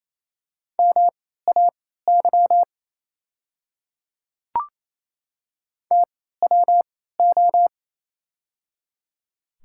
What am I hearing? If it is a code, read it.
Morse code, heard as a keyed tone, about 29 wpm, spaced as for 16 wpm, 690 Hz. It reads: MAY TWO
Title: 3x Character Spacing - Top 100 Words - 28wpm